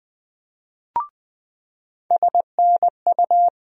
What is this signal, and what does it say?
Morse code, heard as a keyed tone, about 20 wpm, 700 Hz. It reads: SNU
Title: USA Call Sign Postfixes - 3 Letters 20wpm